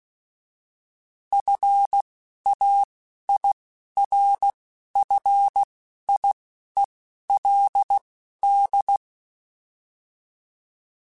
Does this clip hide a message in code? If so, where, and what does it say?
Yes: Morse code from 1.3 to 9.0 s, heard as a keyed tone, about 16 wpm, spaced as for 11 wpm, 800 Hz. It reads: FAIRFIELD